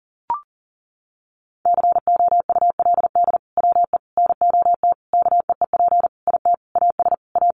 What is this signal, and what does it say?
Morse code, heard as a keyed tone, about 40 wpm, 700 Hz. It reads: COULD WE NOT KEEP IT AS A
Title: Sentences from Top 1200 Words 40wpm